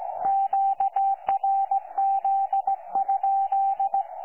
Morse code, heard as a keyed tone, about 17 wpm, 780 Hz. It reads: QRZ?
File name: 12 45Z 18088.1kHz